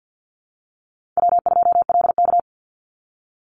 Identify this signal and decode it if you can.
Morse code, heard as a keyed tone, about 50 wpm, 700 Hz. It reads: W2LK
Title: Easy Call Signs Repeated 3x - 50wpm